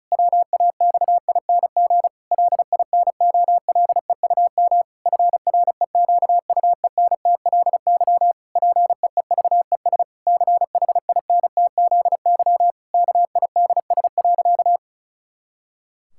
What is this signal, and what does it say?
Morse code, heard as a keyed tone, about 34 wpm, 700 Hz. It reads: WAXING LINOLEUM FREQUENTLY PEEVES CHINTZY KIDS.
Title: Pangram Challenge v2 - 35wpm